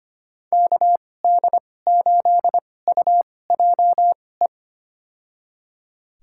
Morse code, heard as a keyed tone, about 25 wpm, spaced as for 18 wpm, 700 Hz. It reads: KD8UJE
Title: USA Call Signs - 6 Characters - 2x Character Spacing - 25wpm